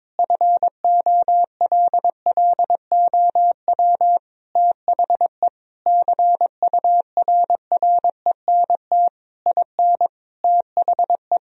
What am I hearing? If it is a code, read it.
Morse code, heard as a keyed tone, about 21 wpm, 700 Hz. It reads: FOLLOW THE CURRENT IN THE
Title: Sentences from Top 1000 Words - Review Part 4 - 22wpm